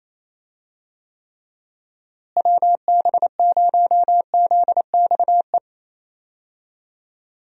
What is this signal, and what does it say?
Morse code, heard as a keyed tone, about 28 wpm, 700 Hz. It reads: WB0ZXE
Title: USA Call Signs - 6 Characters - Repeated 3x - 28wpm